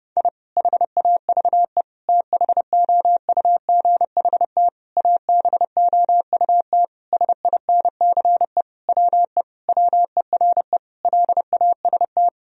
Morse code, heard as a keyed tone, about 30 wpm, 700 Hz. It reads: I HAVE THOUGHT ABOUT SINCE WE WERE LAST
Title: Sentences from Top 1000 Words - Review Part 1 - 30wpm